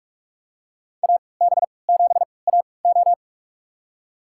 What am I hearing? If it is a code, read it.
Morse code, heard as a keyed tone, about 46 wpm, spaced as for 25 wpm, 700 Hz. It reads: AB7AO
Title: USA Call Signs - 5 Characters - 3x Character Spacing - 45wpm